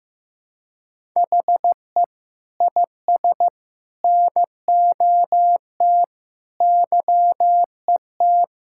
Morse code, heard as a keyed tone, about 16 wpm, 700 Hz. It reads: HE IS NOT YET
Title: Sentences from Top 1000 Words - Review Part 6 - 15wpm